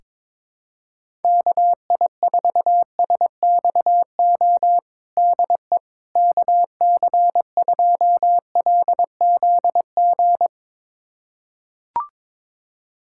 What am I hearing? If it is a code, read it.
Morse code, heard as a keyed tone, about 21 wpm, 700 Hz. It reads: KI4SXO DE KC2LZG
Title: USA Callsign DE Callsign 22wpm